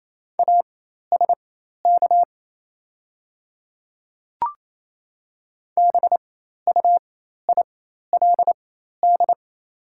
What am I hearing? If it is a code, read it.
Morse code, heard as a keyed tone, about 28 wpm, spaced as for 13 wpm, 700 Hz. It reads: ASK BUILD
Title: 4x Character Spacing - Top 200 Words - 28wpm